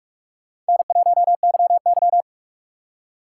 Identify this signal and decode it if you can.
Morse code, heard as a keyed tone, about 56 wpm, 700 Hz. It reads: N1YY